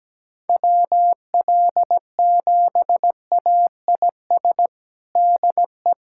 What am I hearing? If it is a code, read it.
Morse code, heard as a keyed tone, about 17 wpm, 700 Hz. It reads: WL7AIS DE
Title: USA Callsign DE Callsign 17wpm